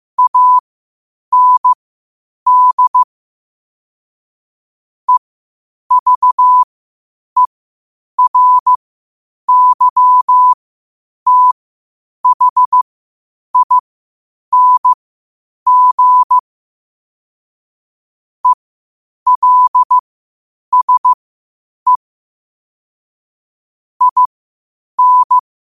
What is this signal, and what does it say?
Morse code, heard as a keyed tone, about 15 wpm, spaced as for 8 wpm, 1000 Hz. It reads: AND EVERYTHING ELSE IN
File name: qotdfiles / 8wpm